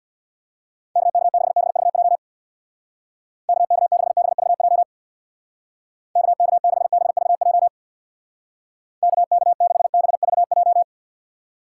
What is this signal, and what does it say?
Morse code, heard as a keyed tone, about 63 wpm, 700 Hz. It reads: KK6BVJ KK6BVJ KK6BVJ KK6BVJ